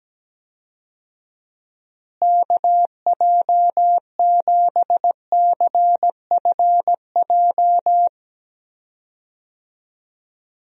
Morse code, heard as a keyed tone, about 17 wpm, 700 Hz. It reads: KJ7CFJ
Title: USA Call Signs - 6 Characters - Repeated 2x - 17wpm